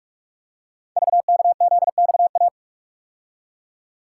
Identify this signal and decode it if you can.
Morse code, heard as a keyed tone, about 44 wpm, 700 Hz. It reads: VKZXA
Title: Code Groups - 5 Letters - 45wpm